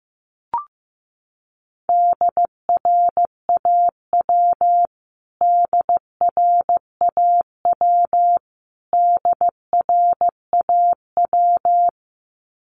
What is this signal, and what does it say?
Morse code, heard as a keyed tone, about 15 wpm, 700 Hz. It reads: DRAW DRAW DRAW
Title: Top 500 Words Repeated 15wpm